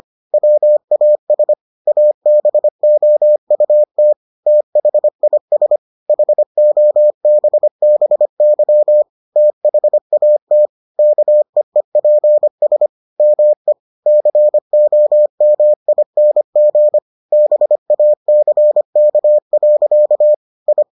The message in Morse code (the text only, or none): WAS ABOUT THIS HOBBY THAT KEEPS ME COMING BACK. I